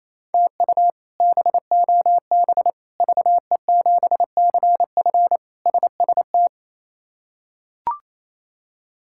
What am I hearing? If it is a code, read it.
Morse code, heard as a keyed tone, about 28 wpm, 700 Hz. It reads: TU BOB VE7CF SST
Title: K1USN SST QSO Practice - 28wpm